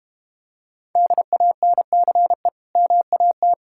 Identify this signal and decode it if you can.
Morse code, heard as a keyed tone, about 32 wpm, 700 Hz. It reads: DANCE MAT